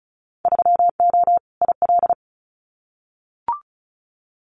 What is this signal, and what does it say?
Morse code, heard as a keyed tone, about 36 wpm, 690 Hz. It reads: 3O IL
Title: Winter Field Day Exchanges 35wpm